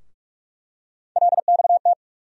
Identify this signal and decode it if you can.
Morse code, heard as a keyed tone, about 46 wpm, 700 Hz. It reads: LXT
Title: USA Call Sign Postfixes - 3 Letters 45wpm